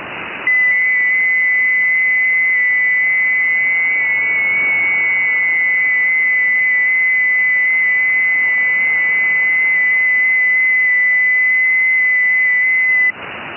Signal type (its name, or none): FT8